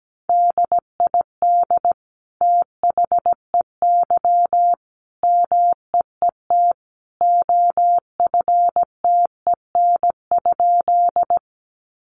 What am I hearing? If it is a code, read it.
Morse code, heard as a keyed tone, about 17 wpm, 700 Hz. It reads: DID THEY MEET OFTEN?
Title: Sentences from Top 1100 Words 17wpm